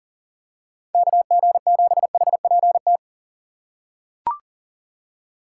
Morse code, heard as a keyed tone, about 40 wpm, 700 Hz. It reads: KG7HPT
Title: USA Call Signs - 4x Character Spacing - 40wpm